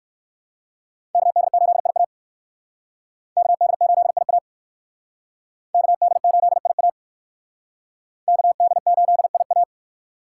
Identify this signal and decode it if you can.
Morse code, heard as a keyed tone, about 59 wpm, 700 Hz. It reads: KD8IA KD8IA KD8IA KD8IA